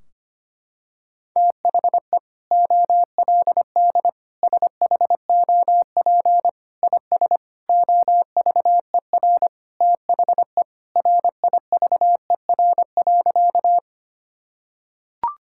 Morse code, heard as a keyed tone, about 25 wpm, 700 Hz. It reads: THE OLD SHOP IS OVER THE RIVER.
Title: Sentences from Top 1600 Words 25wpm